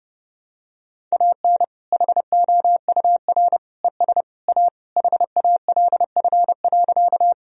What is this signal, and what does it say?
Morse code, heard as a keyed tone, about 30 wpm, 700 Hz. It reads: AN HOUR ES A HALF.
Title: Sentences from Top 400 Words and Abbreviations 30wpm